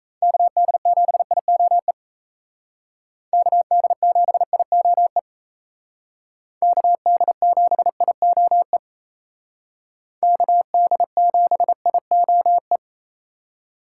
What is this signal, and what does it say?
Morse code, heard as a keyed tone, about 35 wpm, 700 Hz. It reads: KD7IOE KD7IOE KD7IOE KD7IOE
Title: Speed Racing - USA 2x3 Call Signs 28wpm